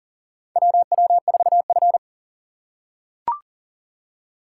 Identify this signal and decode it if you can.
Morse code, heard as a keyed tone, about 40 wpm, 700 Hz. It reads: WW4F
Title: USA Call Signs - 4 Characters - 6x Character Spacing - 40wpm